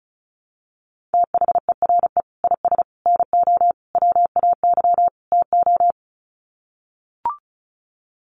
Morse code, heard as a keyed tone, about 34 wpm, 700 Hz. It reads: THERE IS NO WAY TO
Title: Sets of 5 Words Encore 35wpm